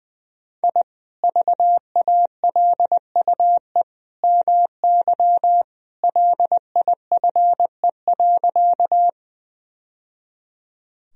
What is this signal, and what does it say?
Morse code, heard as a keyed tone, about 21 wpm, 700 Hz. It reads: I VALUE MY LIFE.
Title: Sentences from Top 1000 Words - Review Part 3 - 20wpm